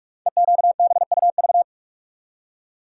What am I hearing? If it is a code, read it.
Morse code, heard as a keyed tone, about 44 wpm, 690 Hz. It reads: EQBUV
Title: Code Groups - 5 Letters - 45wpm